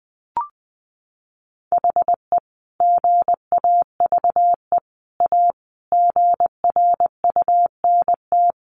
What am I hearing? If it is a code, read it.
Morse code, heard as a keyed tone, about 20 wpm, 700 Hz. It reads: HE GAVE A GRUNT
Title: A Sound Of Thunder 20wpm